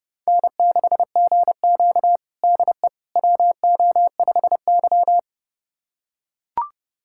Morse code, heard as a keyed tone, about 32 wpm, 700 Hz. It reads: N6GQ DE WO5Y
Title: USA Callsign DE Callsign - 4 Characters - 30wpm